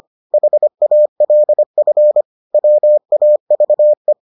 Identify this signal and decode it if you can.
Morse code, heard as a keyed tone, about 25 wpm, 600 Hz. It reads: HALF WAVE